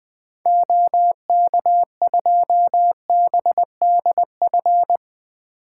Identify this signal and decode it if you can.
Morse code, heard as a keyed tone, about 20 wpm, 700 Hz. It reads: OK2BDF
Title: 2023 SOTA Activator Call Signs - 20wpm